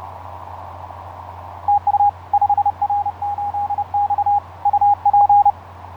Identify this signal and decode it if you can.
Morse code, heard as a keyed tone, about 32 wpm, 820 Hz. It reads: TU 5R8X UP
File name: 5r8x iz5ebl etc 12m cw 07nov2010 1003